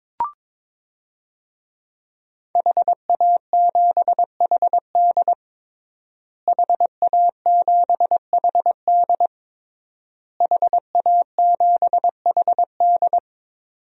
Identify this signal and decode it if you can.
Morse code, heard as a keyed tone, about 22 wpm, 700 Hz. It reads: HA7HD HA7HD HA7HD
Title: DXCC-100 Call Signs - 5 Characters - Repeated 3x - 22wpm